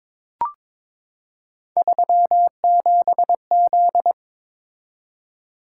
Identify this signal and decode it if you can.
Morse code, heard as a keyed tone, about 21 wpm, 700 Hz. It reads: 37Z
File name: practice / 22wpm / Three Letter-Number Combinations - 22wpm